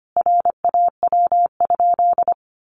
Morse code, heard as a keyed tone, about 25 wpm, 700 Hz. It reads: RAW?
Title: Sentences from Top 1500 Words 25wpm